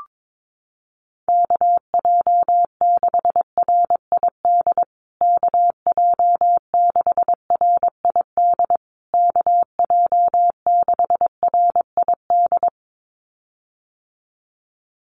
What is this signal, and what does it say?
Morse code, heard as a keyed tone, about 22 wpm, 700 Hz. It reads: KJ6RID KJ6RID KJ6RID
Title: USA Call Signs Repeated 3x 22wpm